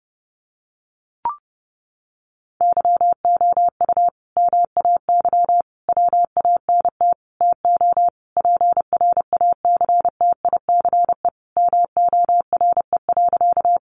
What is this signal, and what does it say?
Morse code, heard as a keyed tone, about 30 wpm, 700 Hz. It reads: YOU MAY WANT TO PRACTICE MORE.
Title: Sentences from Top 700 Words 30wpm